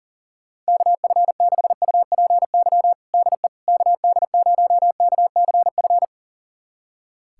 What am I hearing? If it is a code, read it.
Morse code, heard as a keyed tone, about 40 wpm, 700 Hz. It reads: KF6UPY DE KD0KCF
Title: USA Callsign DE Callsign - 6 Characters - 40wpm